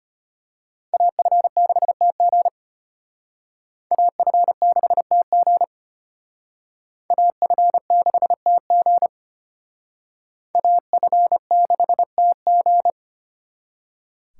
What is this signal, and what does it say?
Morse code, heard as a keyed tone, about 32 wpm, 700 Hz. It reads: AF6TG AF6TG AF6TG AF6TG